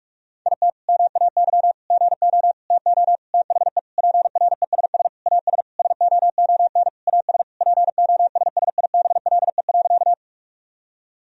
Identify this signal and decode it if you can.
Morse code, heard as a keyed tone, about 46 wpm, 700 Hz. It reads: IT MAY GO TO THE PRESS AS SOON AS POSSIBLE.
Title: Sentences from Top 1000 Words - Review Part 1 - 45wpm